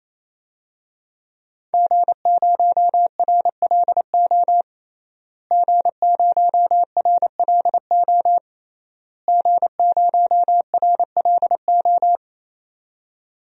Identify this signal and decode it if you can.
Morse code, heard as a keyed tone, about 28 wpm, 700 Hz. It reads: G0RLO G0RLO G0RLO
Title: DXCC-100 Call Signs - 5 Characters - Repeated 3x - 28wpm